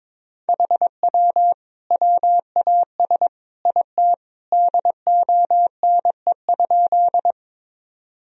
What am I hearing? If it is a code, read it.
Morse code, heard as a keyed tone, about 22 wpm, 700 Hz. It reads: HW WAS IT DONE?
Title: Sentences from Top 500 Words and Abbreviations 22wpm